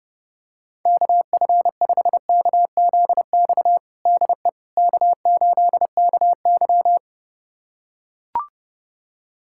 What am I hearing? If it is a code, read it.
Morse code, heard as a keyed tone, about 30 wpm, 700 Hz. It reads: KF5KZX DE K8KY
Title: USA Callsign DE Callsign 30wpm